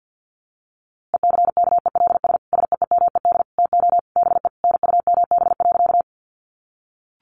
Morse code, heard as a keyed tone, about 50 wpm, 700 Hz. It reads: ECKELS SEEMED TO BE NUMB.
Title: A Sound Of Thunder 50wpm